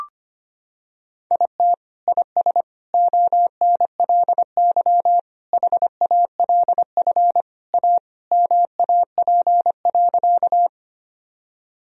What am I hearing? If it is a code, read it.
Morse code, heard as a keyed tone, about 25 wpm, 700 Hz. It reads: IT IS ONLY HALF A MAP.
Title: Sentences from Top 1000 Words - Review Part 2 - 25wpm